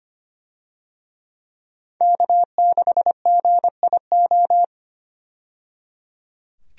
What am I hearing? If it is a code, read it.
Morse code, heard as a keyed tone, about 25 wpm, 700 Hz. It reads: K6GIO
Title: USA Call Signs - 5 Characters - 25wpm